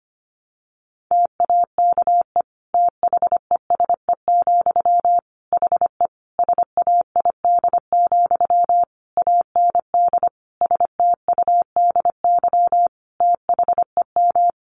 Morse code, heard as a keyed tone, about 25 wpm, 700 Hz. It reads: TAKE THESE, HE SAID, AND STUDY THEM
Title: Sentences from Top 300 Words 25wpm